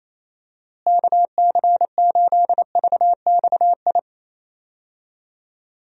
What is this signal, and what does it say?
Morse code, heard as a keyed tone, about 28 wpm, 700 Hz. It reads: KC8VXI
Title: USA Call Signs - 6 Characters - Repeated 3x - 28wpm